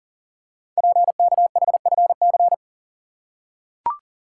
Morse code, heard as a keyed tone, about 38 wpm, 700 Hz. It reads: PKHFC
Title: Code Groups - 5 Letters - 40wpm